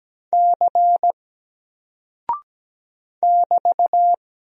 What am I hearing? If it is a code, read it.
Morse code, heard as a keyed tone, about 17 wpm, 700 Hz. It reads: C <BT>
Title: Single Character (Letter:Number:Punctuation:Prosign) Alt 17wpm